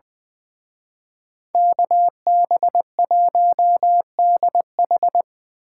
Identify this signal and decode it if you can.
Morse code, heard as a keyed tone, about 24 wpm, 700 Hz. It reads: KB1DH
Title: Speed Racing - USA 2x2 Call Signs 20wpm